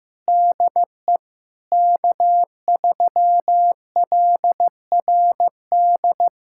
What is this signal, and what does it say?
Morse code, heard as a keyed tone, about 15 wpm, 700 Hz. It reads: DE K3LRD
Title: USA Callsign DE Callsign 15wpm